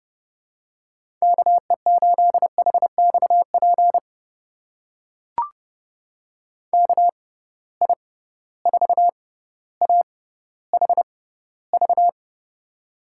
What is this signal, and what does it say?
Morse code, heard as a keyed tone, about 32 wpm, spaced as for 16 wpm, 700 Hz. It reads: KE8HXP KI4AHV
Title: USA Call Signs - 6 Characters - 6x Character Spacing - 30wpm